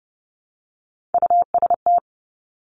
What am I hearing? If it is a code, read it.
Morse code, heard as a keyed tone, about 30 wpm, 700 Hz. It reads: UST
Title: Three Challenging Letter-Number Combinations 30wpm